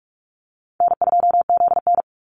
Morse code, heard as a keyed tone, about 44 wpm, 700 Hz. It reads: N2ZN
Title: Easy Call Signs - 45wpm